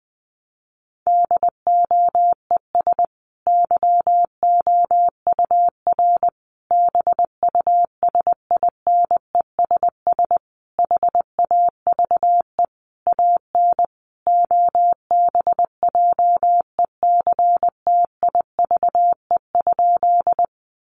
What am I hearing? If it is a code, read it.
Morse code, heard as a keyed tone, about 20 wpm, 700 Hz. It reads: DOES YOUR BUSINESS HAVE AN OBJECTIVE?
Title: Sentences from Top 1200 Words 20wpm